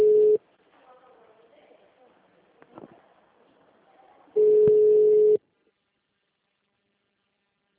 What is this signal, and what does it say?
Ringback tone (Europe, 425 Hz), with speech.
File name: out- -2007- - - .12101